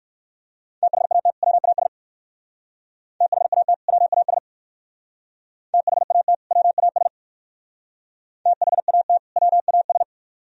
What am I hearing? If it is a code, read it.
Morse code, heard as a keyed tone, about 57 wpm, 700 Hz. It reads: THAT WAS THAT WAS THAT WAS THAT WAS